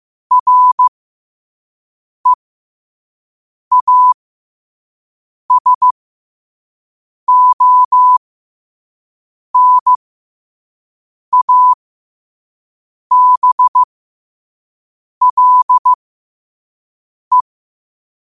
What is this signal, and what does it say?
Morse code, heard as a keyed tone, about 15 wpm, spaced as for 5 wpm, 1000 Hz. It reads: REASONABLE
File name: qotdfiles / 5wpm